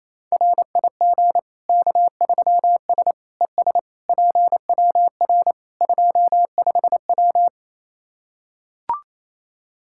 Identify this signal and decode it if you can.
Morse code, heard as a keyed tone, about 28 wpm, 700 Hz. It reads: RIG K3S ES PWR 25W
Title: Practice QSO - Exchange 2 - Info - 28wpm